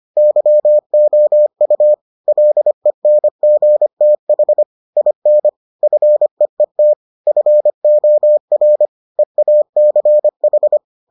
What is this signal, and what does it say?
Morse code, heard as a keyed tone, about 25 wpm, 600 Hz. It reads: YOU LENGTH IN FEET FOR EACH